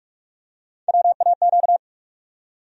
Morse code, heard as a keyed tone, about 46 wpm, 700 Hz. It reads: WAQ
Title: USA Call Sign Postfixes - 3 Letters 45wpm